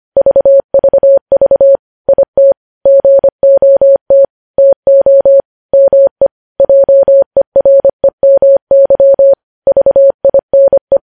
Morse code, heard as a keyed tone, about 25 wpm, 550 Hz. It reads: VVV IT GOT TO ME JEREMY VINE
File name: podcast / News headlines-1x-25x25